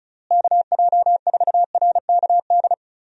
Morse code, heard as a keyed tone, about 36 wpm, 700 Hz. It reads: KJ4RKD
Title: USA Call Signs - 6 Characters - Repeated 3x - 35wpm